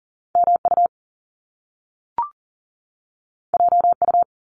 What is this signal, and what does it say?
Morse code, heard as a keyed tone, about 39 wpm, 700 Hz. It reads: MU JU
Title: Top Two Letter Combinations in Words 40wpm